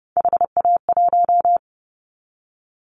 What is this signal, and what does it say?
Morse code, heard as a keyed tone, about 30 wpm, 700 Hz. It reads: HA1
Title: Callsign Prefixes for Top 50 DXCC - 30wpm